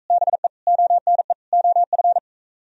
Morse code, heard as a keyed tone, about 42 wpm, 700 Hz. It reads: BE ONE OF